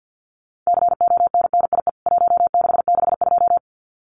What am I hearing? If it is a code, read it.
Morse code, heard as a keyed tone, about 50 wpm, 700 Hz. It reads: CONNIE 1662
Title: CWT Exchange 50wpm